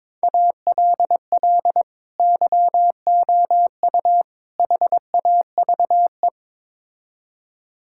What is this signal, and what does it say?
Morse code, heard as a keyed tone, about 22 wpm, 700 Hz. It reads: ALL YOU HAVE
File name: practice / 22wpm / Build A Phrase - 5-Word Phrases - 22wpm